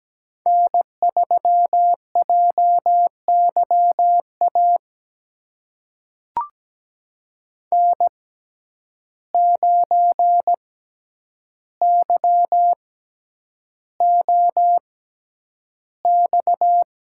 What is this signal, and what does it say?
Morse code, heard as a keyed tone, about 17 wpm, spaced as for 9 wpm, 700 Hz. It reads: N3JYA N9YOX
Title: USA Call Signs - 5 Characters - 6x Character Spacing - 17wpm